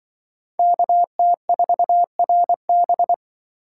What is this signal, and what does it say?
Morse code, heard as a keyed tone, about 24 wpm, 700 Hz. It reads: KT4RB
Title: Speed Racing - USA Call Signs 20wpm